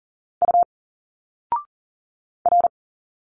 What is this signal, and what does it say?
Morse code, heard as a keyed tone, about 40 wpm, 700 Hz. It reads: U R
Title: Single Letters 40wpm